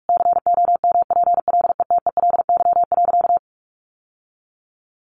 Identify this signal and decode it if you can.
Morse code, heard as a keyed tone, about 44 wpm, 700 Hz. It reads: COMPLETELY.
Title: Sentences from Top 1100 Words 45wpm